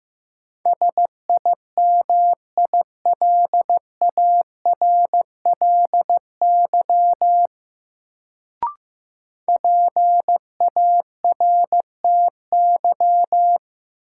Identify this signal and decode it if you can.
Morse code, heard as a keyed tone, about 15 wpm, 700 Hz. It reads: SIMILARLY PARTY
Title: Sentences from Top 1400 Words - New Word Drill - 15wpm